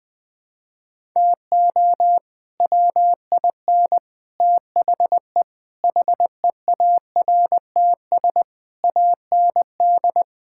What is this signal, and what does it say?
Morse code, heard as a keyed tone, about 21 wpm, 700 Hz. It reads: TO WIN THE HEARTS AND
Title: Sets of 8 Words - 20wpm